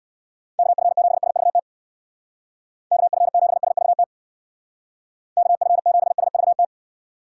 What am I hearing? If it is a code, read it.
Morse code, heard as a keyed tone, about 65 wpm, 700 Hz. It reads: KF7SVT KF7SVT KF7SVT